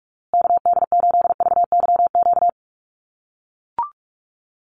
Morse code, heard as a keyed tone, about 44 wpm, 700 Hz. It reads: KD8VYQ
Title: USA Call Signs Repeated 2x 45wpm